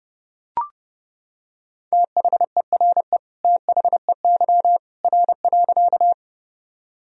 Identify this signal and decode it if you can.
Morse code, heard as a keyed tone, about 30 wpm, 700 Hz. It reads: THERE THEY R.
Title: Sentences from Top 100 Words and Abbreviations 30wpm